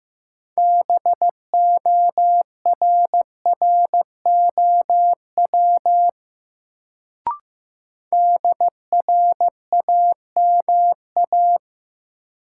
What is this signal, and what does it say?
Morse code, heard as a keyed tone, about 15 wpm, 700 Hz. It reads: BORROW DRAMA
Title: Sentences from Top 1500 Words - New Word Drill - 15wpm